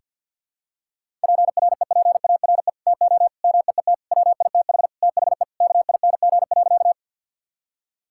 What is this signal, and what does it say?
Morse code, heard as a keyed tone, about 52 wpm, 700 Hz. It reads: PREPARE TO MEET WITH THE KING.
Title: Sentences from Top 1000 Words - Review Part 6 - 50wpm